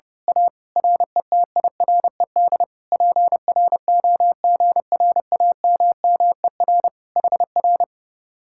Morse code, heard as a keyed tone, about 29 wpm, 700 Hz. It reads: A RETIRED PROGRAMMER HR
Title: Speed-Racing Common QSO Phrases - 30wpm